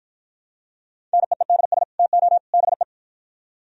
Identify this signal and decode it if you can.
Morse code, heard as a keyed tone, about 56 wpm, 700 Hz. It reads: NEEDS TO BE